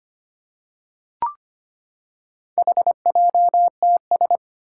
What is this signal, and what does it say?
Morse code, heard as a keyed tone, about 25 wpm, 700 Hz. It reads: HJTS
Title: Four Challenging Letter-Number Combinations 25wpm